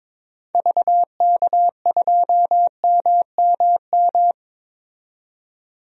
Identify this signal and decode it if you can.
Morse code, heard as a keyed tone, about 21 wpm, 700 Hz. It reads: VK2MMM
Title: DXCC-100 Call Signs - 6 Characters - 22wpm